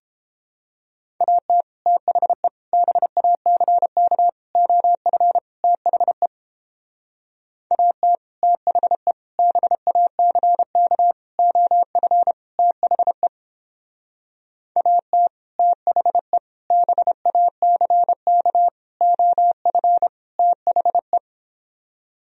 Morse code, heard as a keyed tone, about 29 wpm, 700 Hz. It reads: AT THE BACK OF THE AT THE BACK OF THE AT THE BACK OF THE